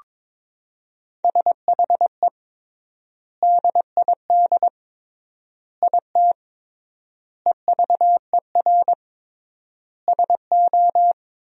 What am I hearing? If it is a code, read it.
Morse code, heard as a keyed tone, about 22 wpm, 700 Hz. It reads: SHE DID IT EVER SO
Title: 3x Word-Spacing - Sentences from Top 400 Words - 22wpm